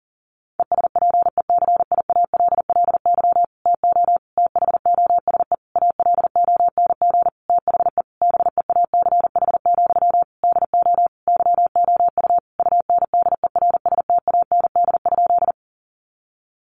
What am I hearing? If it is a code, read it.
Morse code, heard as a keyed tone, about 40 wpm, 700 Hz. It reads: ESPECIALLY TO THOSE ALONG THE BEACH, DO YOU UNDERSTAND?
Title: Sentences from Top 1500 Words 40wpm